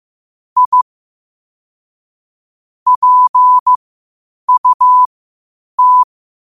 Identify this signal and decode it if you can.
Morse code, heard as a keyed tone, about 15 wpm, spaced as for 8 wpm, 1000 Hz. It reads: I PUT
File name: qotdfiles / 8wpm